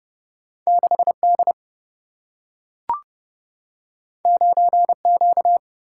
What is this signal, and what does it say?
Morse code, heard as a keyed tone, about 30 wpm, 700 Hz. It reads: 6 D 9 Q
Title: Single-Letter Single-Number 30wpm